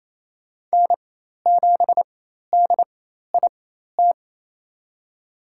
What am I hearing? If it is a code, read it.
Morse code, heard as a keyed tone, about 28 wpm, spaced as for 13 wpm, 700 Hz. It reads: N7DIT
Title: USA Call Signs - 5 Characters - 4x Character Spacing - 28wpm